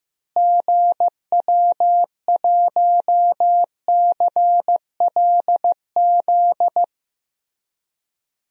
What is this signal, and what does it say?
Morse code, heard as a keyed tone, about 15 wpm, 700 Hz. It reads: GW1CLZ
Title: DXCC-100 Call Signs - 6 Characters - Repeated 3x - 15wpm